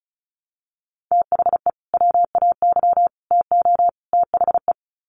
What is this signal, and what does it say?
Morse code, heard as a keyed tone, about 34 wpm, 700 Hz. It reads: THE WAY TO THE
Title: Sets of 4 Words 35wpm